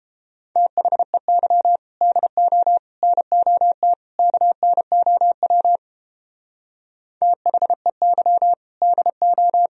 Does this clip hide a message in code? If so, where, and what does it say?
Yes: Morse code from 0.6 to 9.7 s, heard as a keyed tone, about 29 wpm, 700 Hz. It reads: THEY DO NOT KNOW THEY DO